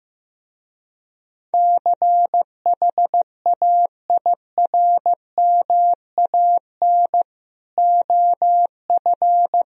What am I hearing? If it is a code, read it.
Morse code, heard as a keyed tone, about 16 wpm, 700 Hz. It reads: CHAIRMAN OF
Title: Speed-Racing - Sets of 3 Words 15wpm